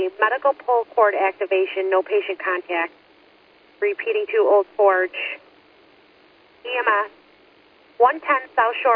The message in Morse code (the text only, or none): none